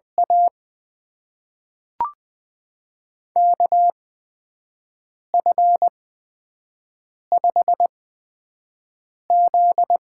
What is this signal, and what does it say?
Morse code, heard as a keyed tone, about 20 wpm, spaced as for 5 wpm, 700 Hz. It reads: A KF5Z
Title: USA Call Signs - 8x Character Spacing - 20wpm